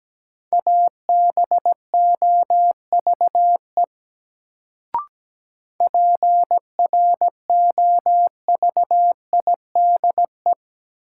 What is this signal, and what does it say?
Morse code, heard as a keyed tone, about 17 wpm, 700 Hz. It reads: ABOVE PROVIDE
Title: Sentences from Top 1000 Words - Review Part 2 - New Word Drill - 17wpm